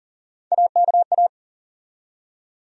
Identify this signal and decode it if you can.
Morse code, heard as a keyed tone, about 40 wpm, 700 Hz. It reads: AKA
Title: USA Call Sign Postfixes - 3 Letters 40wpm